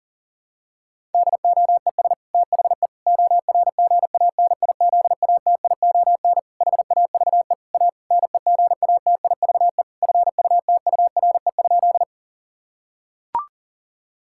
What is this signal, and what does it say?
Morse code, heard as a keyed tone, about 40 wpm, 700 Hz. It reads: DOES THE ORGANIZATION HAVE A NEGATIVE FUTURE?
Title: Sentences from Top 1000 Words - Review Part 6 - 40wpm